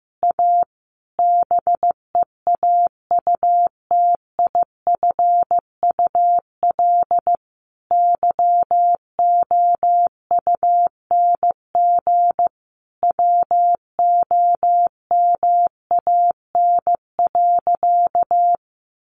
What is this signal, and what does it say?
Morse code, heard as a keyed tone, about 15 wpm, 700 Hz. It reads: A BEAUTIFUL YOUNG WOMAN.
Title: Sentences from Top 1100 Words 15wpm